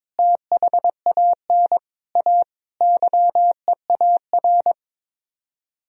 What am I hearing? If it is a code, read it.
Morse code, heard as a keyed tone, about 22 wpm, 700 Hz. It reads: THAN A YEAR
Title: Speed-Racing - Sets of 5 Words 22wpm